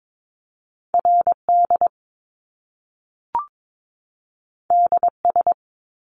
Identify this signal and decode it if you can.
Morse code, heard as a keyed tone, about 22 wpm, 700 Hz. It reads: RD DS
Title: Top Two Letter Combinations in Words 22wpm